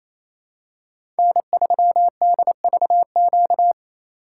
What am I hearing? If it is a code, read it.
Morse code, heard as a keyed tone, about 28 wpm, 700 Hz. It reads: N3DVQ
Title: USA Call Signs - 2x Character Spacing - 28wpm